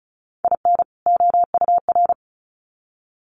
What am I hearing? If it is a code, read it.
Morse code, heard as a keyed tone, about 34 wpm, 700 Hz. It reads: IN OUR
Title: Sets of 2 Words 35wpm